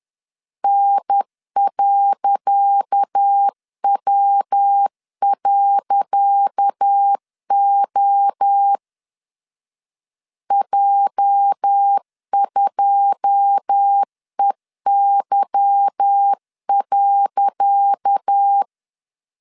Morse code, heard as a keyed tone, about 10 wpm, 800 Hz. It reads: N.W.O J2EY.
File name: LEZIONI / 10 SIGNS / 10C DOT 10wpm 2940